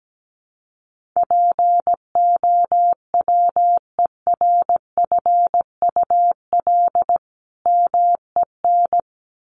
Morse code, heard as a keyed tone, about 17 wpm, 700 Hz. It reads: POWERFUL MEN
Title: Sentences from Top 1400 Words 17wpm